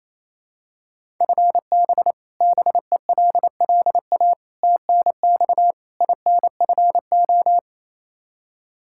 Morse code, heard as a keyed tone, about 28 wpm, 700 Hz. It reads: FB BELLA TNX INFO
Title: Practice QSO - Exchange 3 - More Info - 28wpm